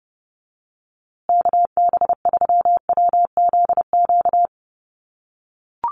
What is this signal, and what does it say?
Morse code, heard as a keyed tone, about 30 wpm, 700 Hz. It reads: KB3WZQ
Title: USA Call Signs Repeated 3x 30wpm